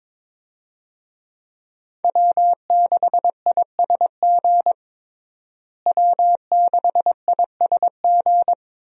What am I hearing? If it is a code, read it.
Morse code, heard as a keyed tone, about 22 wpm, 700 Hz. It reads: W6ISG W6ISG
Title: USA Call Signs - 5 Characters - Repeated 2x - 22wpm